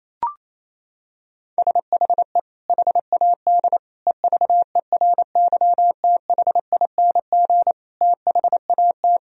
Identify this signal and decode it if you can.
Morse code, heard as a keyed tone, about 28 wpm, 700 Hz. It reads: SHE HAD EVERYTHING THAT
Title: Sentences from Top 1000 Words - Review Part 4 - 28wpm